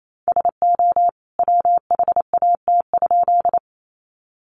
Morse code, heard as a keyed tone, about 28 wpm, 700 Hz. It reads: SO WHAT?
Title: A Sound Of Thunder 28wpm